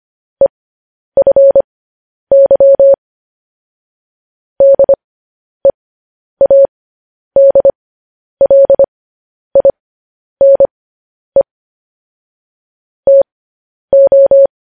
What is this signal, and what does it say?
Morse code, heard as a keyed tone, about 25 wpm, spaced as for 10 wpm, 550 Hz. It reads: EFY DEADLINE TO